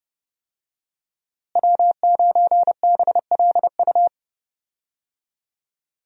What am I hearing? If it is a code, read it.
Morse code, heard as a keyed tone, about 30 wpm, 700 Hz. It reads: W9BLU
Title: USA Call Signs - 5 Characters - Repeated 2x - 30wpm